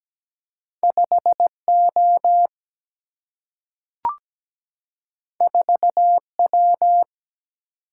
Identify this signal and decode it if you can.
Morse code, heard as a keyed tone, about 17 wpm, 700 Hz. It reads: 5O 4W
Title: Single-Letter Single-Number 17wpm